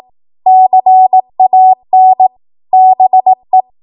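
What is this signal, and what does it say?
Morse code, heard as a keyed tone, about 18 wpm, 750 Hz. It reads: CAN BE
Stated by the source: 18 WPM Morse code practice from a national amateur radio association's code archive.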